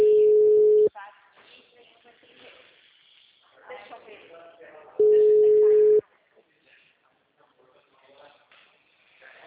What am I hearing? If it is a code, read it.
Ringback tone (Europe, 425 Hz), with speech.